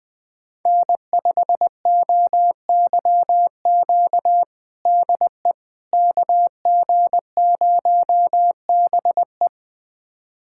Morse code, heard as a keyed tone, about 20 wpm, 700 Hz. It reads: N5OYQ DE KG0BE
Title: USA Callsign DE Callsign 20wpm